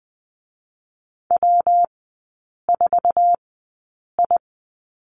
Morse code, heard as a keyed tone, about 20 wpm, 700 Hz. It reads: W 4 I
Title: 114 Lesson 19e - Letter w - Three Characters 20wpm